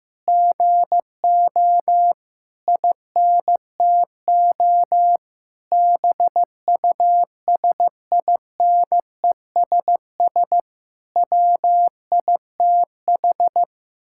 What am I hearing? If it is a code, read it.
Morse code, heard as a keyed tone, about 15 wpm, 700 Hz. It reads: GO INTO BUSINESS WITH
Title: Sentences from Top 1000 Words - Review Part 6 - 15wpm